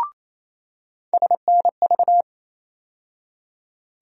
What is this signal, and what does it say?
Morse code, heard as a keyed tone, about 28 wpm, 700 Hz. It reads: SNV
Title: USA Call Sign Postfixes - 3 Letters 28wpm